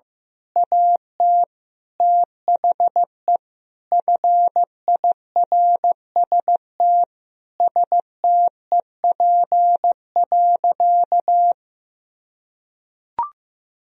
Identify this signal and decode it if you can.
Morse code, heard as a keyed tone, about 15 wpm, 700 Hz. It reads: AT THE FIRST STEP.
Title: Sentences from Top 1000 Words - Review Part 2 - 15wpm